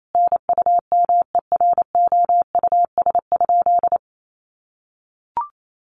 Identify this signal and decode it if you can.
Morse code, heard as a keyed tone, about 28 wpm, 700 Hz. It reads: NUMEROUS?
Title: Sentences from Top 1400 Words 28wpm